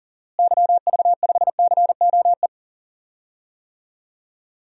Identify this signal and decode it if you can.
Morse code, heard as a keyed tone, about 40 wpm, 700 Hz. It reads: YV5COE
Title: DXCC-100 Call Signs - 6 Characters - Repeated 3x - 40wpm